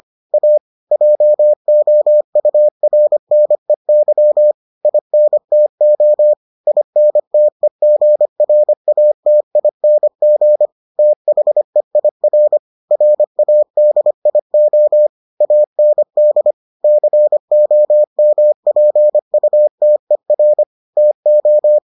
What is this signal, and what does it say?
Morse code, heard as a keyed tone, about 25 wpm, 600 Hz. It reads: A JOURNEY INTO INTEGRATING THEIR RADIO AND COMPUTER TO